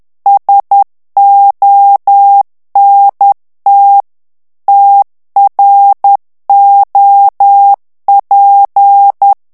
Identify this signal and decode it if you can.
Morse code, heard as a keyed tone, about 10 wpm, 800 Hz. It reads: SONT TROP
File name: FRANCAIS TEXTE CLAIR 2 VITESSE 10 MOTS MINUTES